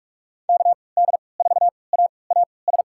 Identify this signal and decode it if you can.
Morse code, heard as a keyed tone, about 46 wpm, spaced as for 26 wpm, 690 Hz. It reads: KD4AAS
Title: USA Call Signs - 6 Characters - 3x Character Spacing - 45wpm